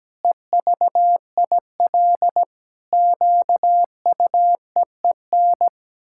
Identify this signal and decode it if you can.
Morse code, heard as a keyed tone, about 17 wpm, 700 Hz. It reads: EVIL QUEEN
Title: Pangram Challenge v2 - 17wpm